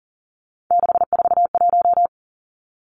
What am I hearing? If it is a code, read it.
Morse code, heard as a keyed tone, about 42 wpm, 700 Hz. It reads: 641
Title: Three Challenging Letter-Number Combinations 40wpm